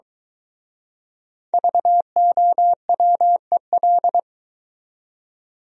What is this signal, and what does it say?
Morse code, heard as a keyed tone, about 23 wpm, 700 Hz. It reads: VOWEL